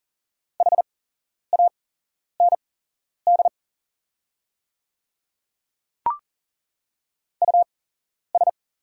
Morse code, heard as a keyed tone, about 40 wpm, spaced as for 11 wpm, 690 Hz. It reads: HAND US
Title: 8x Character Spacing - Top 300 Words - 40wpm